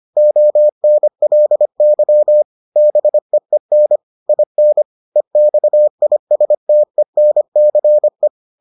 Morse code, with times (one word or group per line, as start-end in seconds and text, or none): 0.2-2.4 ONLY
2.8-4.0 BEEN
4.3-4.8 IN
5.2-8.3 EXISTENCE